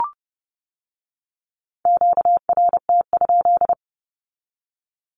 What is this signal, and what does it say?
Morse code, heard as a keyed tone, about 30 wpm, 700 Hz. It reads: QRT?
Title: Q-Codes with Spoken Meaning - 30wpm